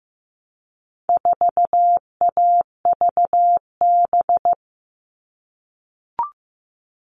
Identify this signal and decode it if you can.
Morse code, heard as a keyed tone, about 15 wpm, 700 Hz. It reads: 4AVB
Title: All Cut Numbers - 4-Digit 15wpm